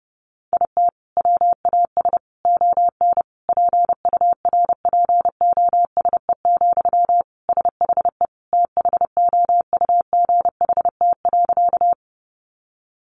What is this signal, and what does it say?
Morse code, heard as a keyed tone, about 30 wpm, 700 Hz. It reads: IT WAS ON PURPOSE, SHE THOUGHT.
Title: Sentences from Top 800 Words 30wpm